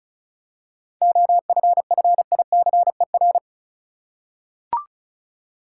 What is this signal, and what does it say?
Morse code, heard as a keyed tone, about 34 wpm, 700 Hz. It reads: OFFICER
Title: Seven Letter Words - 35wpm